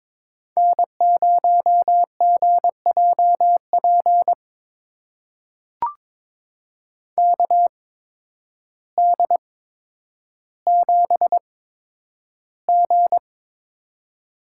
Morse code, heard as a keyed tone, about 22 wpm, spaced as for 10 wpm, 700 Hz. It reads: N0GJP KD7G
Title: USA Call Signs - 8x Character Spacing - 22wpm